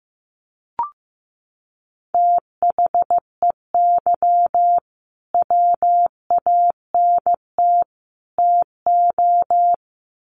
Morse code, heard as a keyed tone, about 15 wpm, 700 Hz. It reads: THEY WANT TO
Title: Sets of 3 Words 15wpm